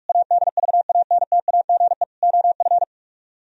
Morse code, heard as a keyed tone, about 61 wpm, spaced as for 53 wpm, 700 Hz. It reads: ADVANTAGE OF
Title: Speed-Racing - Sets of 4 Words 45wpm